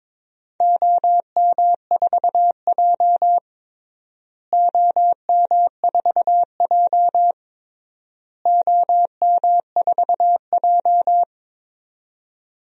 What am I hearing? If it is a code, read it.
Morse code, heard as a keyed tone, about 22 wpm, 700 Hz. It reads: OM4J OM4J OM4J
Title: DXCC-100 Call Signs - 4 Characters - Repeated 3x - 22wpm